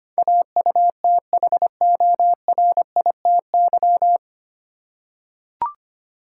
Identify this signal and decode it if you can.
Morse code, heard as a keyed tone, about 25 wpm, 690 Hz. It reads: AUTHORITY
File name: practice / 25wpm / 9 Letter Words - First 3 Letters Given - 25wpm